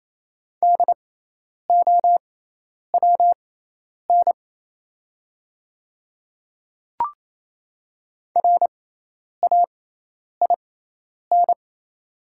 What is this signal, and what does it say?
Morse code, heard as a keyed tone, about 28 wpm, spaced as for 10 wpm, 700 Hz. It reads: DOWN RAIN
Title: 6x Character Spacing - Top 300 Words - 28wpm